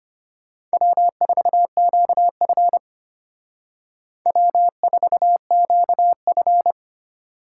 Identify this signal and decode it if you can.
Morse code, heard as a keyed tone, about 32 wpm, 700 Hz. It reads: W4QF W4QF